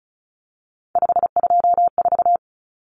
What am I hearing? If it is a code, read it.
Morse code, heard as a keyed tone, about 34 wpm, 700 Hz. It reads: 524
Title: Common Cut Numbers - 3-Digit 35wpm